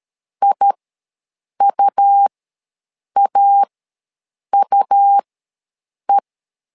Morse code, heard as a keyed tone, about 13 wpm, spaced as for 7 wpm, 800 Hz. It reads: IUAUE
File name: LEZIONI / 2 UHDN / 2 U ws 12wpm 1655